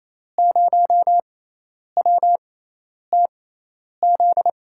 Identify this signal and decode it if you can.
Morse code, heard as a keyed tone, about 28 wpm, spaced as for 10 wpm, 700 Hz. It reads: 0WTZ
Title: USA Call Signs - 6 Characters - 6x Character Spacing - 28wpm